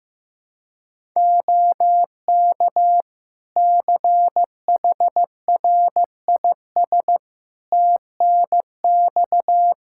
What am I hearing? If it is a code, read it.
Morse code, heard as a keyed tone, about 15 wpm, 700 Hz. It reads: OK CHRIS TNX
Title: Practice QSO - Exchange 4 - Ending - 15wpm